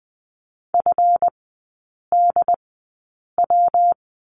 Morse code, heard as a keyed tone, about 20 wpm, spaced as for 8 wpm, 700 Hz. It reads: FDW
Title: 152 Lesson 25e - Letter p - Three Characters 20wpm